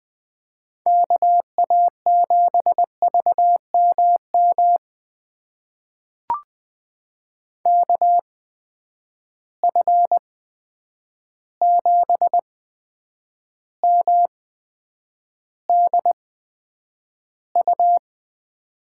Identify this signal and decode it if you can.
Morse code, heard as a keyed tone, about 20 wpm, spaced as for 9 wpm, 700 Hz. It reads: KA7VMM KF7MDU
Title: USA Call Signs - 6 Characters - 8x Character Spacing - 20wpm